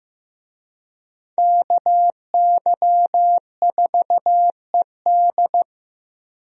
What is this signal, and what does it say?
Morse code, heard as a keyed tone, about 15 wpm, 700 Hz. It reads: KY4ED
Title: USA Call Signs - 5 Characters - Repeated 2x - 15wpm